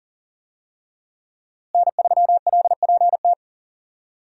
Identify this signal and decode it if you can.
Morse code, heard as a keyed tone, about 40 wpm, 700 Hz. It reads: N3LPT
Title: USA Call Signs - 5 Characters - Repeated 2x - 40wpm